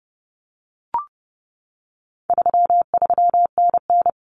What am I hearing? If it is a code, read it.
Morse code, heard as a keyed tone, about 30 wpm, 700 Hz. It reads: 33NN
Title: Common Cut Numbers - 4-Digit 30wpm